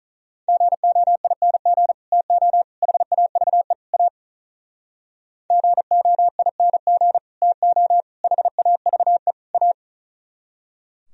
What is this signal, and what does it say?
Morse code, heard as a keyed tone, about 44 wpm, 700 Hz. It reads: GOING TO HAVE A GOING TO HAVE A